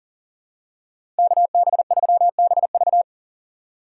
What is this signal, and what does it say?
Morse code, heard as a keyed tone, about 40 wpm, 700 Hz. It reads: KB3BV
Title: USA Call Signs - 5 Characters - 2x Character Spacing - 40wpm